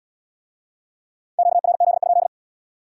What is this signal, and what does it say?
Morse code, heard as a keyed tone, about 79 wpm, 700 Hz. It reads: QRZ?